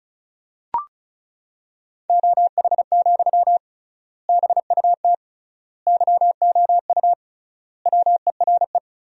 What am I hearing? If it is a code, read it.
Morse code, heard as a keyed tone, about 34 wpm, 700 Hz. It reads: OH, BUT YOU WERE
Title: 3x Word-Spacing - Sentences from Top 500 Words - 35wpm